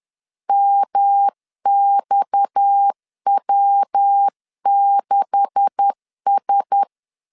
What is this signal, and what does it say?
Morse code, heard as a keyed tone, about 10 wpm, 800 Hz. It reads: MXW6S
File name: LEZIONI / 9 286 10wpm 2735